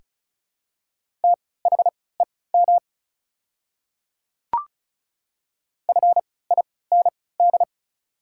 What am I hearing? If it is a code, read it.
Morse code, heard as a keyed tone, about 36 wpm, spaced as for 20 wpm, 700 Hz. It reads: THEM FIND
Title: 3x Character Spacing - Top 200 Words - 35wpm